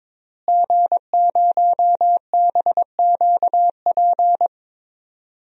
Morse code, heard as a keyed tone, about 22 wpm, 700 Hz. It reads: G0BQP
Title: DXCC-100 Call Signs - 5 Characters - Repeated 3x - 22wpm